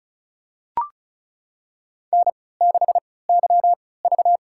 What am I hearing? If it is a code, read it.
Morse code, heard as a keyed tone, about 34 wpm, spaced as for 20 wpm, 700 Hz. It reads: N6YV
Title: USA Call Signs - 4 Characters - 3x Character Spacing - 35wpm